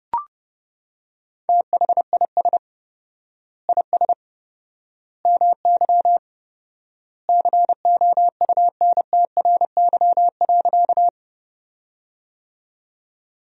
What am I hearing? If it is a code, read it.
Morse code, heard as a keyed tone, about 30 wpm, 700 Hz. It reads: THIS IS MY COUNTRY.
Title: 4x Word-Spacing - Sentences from Top 300 Words - 30wpm